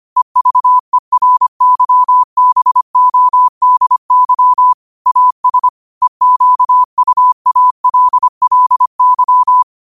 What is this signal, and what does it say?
Morse code, heard as a keyed tone, about 25 wpm, 1000 Hz. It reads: EVERYBODY AS EQUALLY
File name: qotdfiles / 25wpm